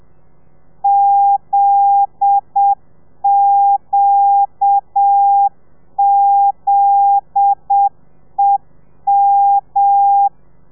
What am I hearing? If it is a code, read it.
Morse code, heard as a keyed tone, about 7 wpm, 800 Hz. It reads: ZQZEM